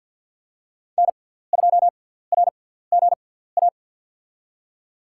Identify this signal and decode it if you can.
Morse code, heard as a keyed tone, about 47 wpm, spaced as for 17 wpm, 700 Hz. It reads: N2RGA
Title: USA Call Signs - 5 Characters - 6x Character Spacing - 50wpm